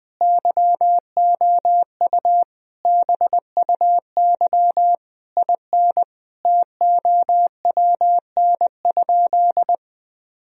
Morse code, heard as a keyed tone, about 21 wpm, 700 Hz. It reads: YOU BUY IN TOWN?
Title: Sentences from Top 1000 Words - Review Part 4 - 20wpm